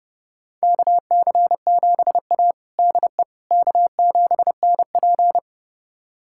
Morse code, heard as a keyed tone, about 32 wpm, 700 Hz. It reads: KC7A DE K7NP
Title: USA Callsign DE Callsign - 4 Characters - 30wpm